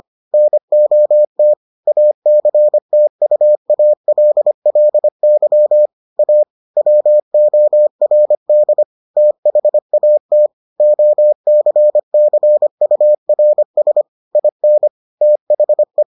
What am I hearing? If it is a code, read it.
Morse code, heard as a keyed tone, about 25 wpm, 600 Hz. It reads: NOT ACTUALLY A WORD THAT OCCURS IN THE